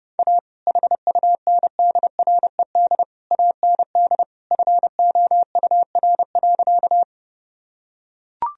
Morse code, heard as a keyed tone, about 32 wpm, 700 Hz. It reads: A HUNDRED AND FOUR.
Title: Sentences from Top 1000 Words - Review Part 2 - 30wpm